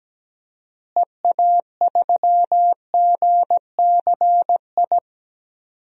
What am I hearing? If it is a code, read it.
Morse code, heard as a keyed tone, about 17 wpm, 700 Hz. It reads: EA3GCI
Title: DXCC-100 Call Signs - 6 Characters - Repeated 2x - 17wpm